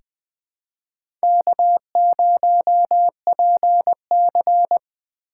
Morse code, heard as a keyed tone, about 21 wpm, 700 Hz. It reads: K0PC
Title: K1USN SST QSO Practice - 20wpm